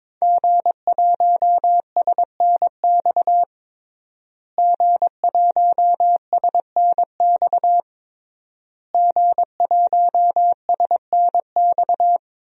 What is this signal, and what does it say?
Morse code, heard as a keyed tone, about 21 wpm, 700 Hz. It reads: G1SNX G1SNX G1SNX
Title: DXCC-100 Call Signs - 5 Characters - Repeated 3x - 22wpm